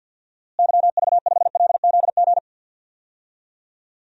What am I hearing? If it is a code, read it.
Morse code, heard as a keyed tone, about 51 wpm, 700 Hz. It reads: YV5LZG
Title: DXCC-100 Call Signs - 6 Characters - Repeated 3x - 50wpm